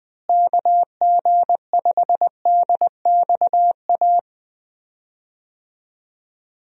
Morse code, heard as a keyed tone, about 21 wpm, 700 Hz. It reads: KG5DXA
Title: USA Call Signs - 6 Characters - Repeated 2x - 20wpm